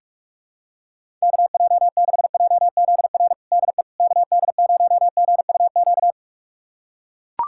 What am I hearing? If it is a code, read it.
Morse code, heard as a keyed tone, about 43 wpm, 700 Hz. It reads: KJ6JZR DE KD0GUQ
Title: USA Callsign DE Callsign 45wpm